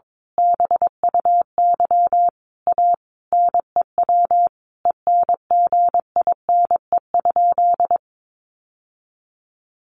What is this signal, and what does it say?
Morse code, heard as a keyed tone, about 22 wpm, 700 Hz. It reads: BUY A NEW ENGINE?
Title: Sentences from Top 1500 Words 22wpm